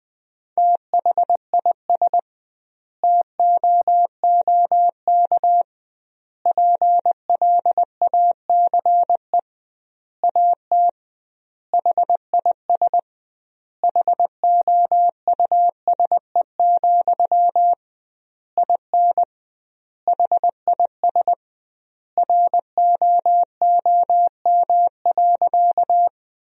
Morse code, heard as a keyed tone, about 19 wpm, 700 Hz. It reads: THIS TOOK PLACE AT HIS HOUSE, IN HIS ROOM.
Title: 2x Word-Spacing - Sentences from Top 300 Words - 20wpm